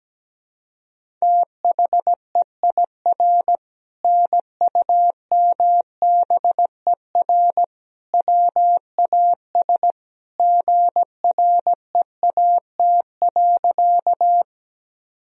Morse code, heard as a keyed tone, about 17 wpm, 700 Hz. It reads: THEIR NUMBER WAS GREAT.
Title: Sentences from Top 1000 Words - Review Part 1 - 17wpm